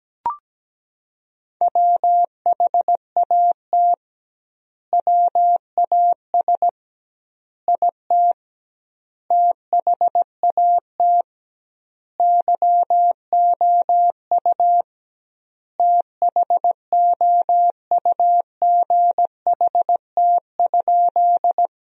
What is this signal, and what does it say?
Morse code, heard as a keyed tone, about 17 wpm, 700 Hz. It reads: WHAT WAS IT THAT YOU THOUGHT?
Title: 2x Word-Spacing - Sentences from Top 300 Words - 17wpm